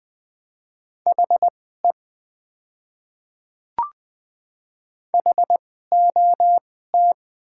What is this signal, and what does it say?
Morse code, heard as a keyed tone, about 20 wpm, spaced as for 15 wpm, 700 Hz. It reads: HE HOT
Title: 2x Character Spacing - Top 200 Words - 20wpm